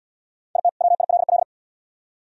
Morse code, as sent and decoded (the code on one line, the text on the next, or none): .. - / .-- .. .-.. .-..
IT WILL